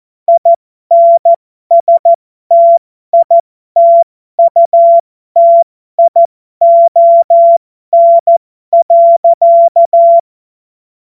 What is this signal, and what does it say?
Morse code, heard as a keyed tone, about 14 wpm, spaced as for 12 wpm, 680 Hz. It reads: INSTITUTION.
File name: Morsekurs / Witze